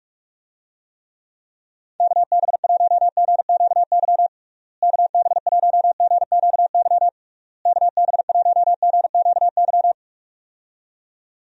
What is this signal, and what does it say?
Morse code, heard as a keyed tone, about 46 wpm, 700 Hz. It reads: KB1GQY KB1GQY KB1GQY
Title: USA Call Signs - 6 Characters - Repeated 3x - 45wpm